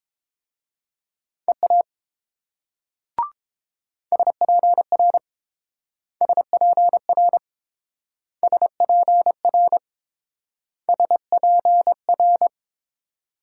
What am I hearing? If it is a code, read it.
Morse code, heard as a keyed tone, about 28 wpm, 700 Hz. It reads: EA SPR SPR SPR SPR